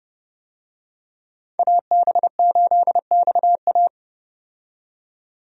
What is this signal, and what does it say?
Morse code, heard as a keyed tone, about 30 wpm, 700 Hz. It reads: AB8XA
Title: USA Call Signs - 5 Characters - 30wpm